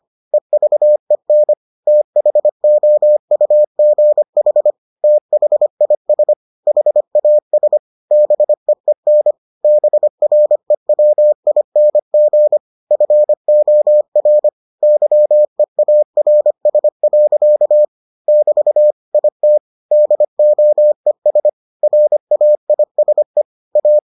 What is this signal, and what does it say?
Morse code, heard as a keyed tone, about 25 wpm, 600 Hz. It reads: EVEN THOUGH THIS HAS BEEN BREWING FOR YEARS. <BT> IT DOES RAISE A